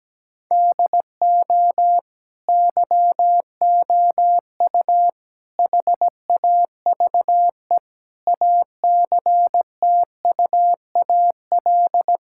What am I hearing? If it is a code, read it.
Morse code, heard as a keyed tone, about 17 wpm, 700 Hz. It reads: DO YOU HAVE ACTUAL
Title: Sentences from Top 1000 Words - Review Part 5 - 17wpm